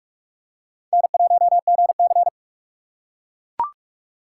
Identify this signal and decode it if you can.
Morse code, heard as a keyed tone, about 40 wpm, 700 Hz. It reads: N1GC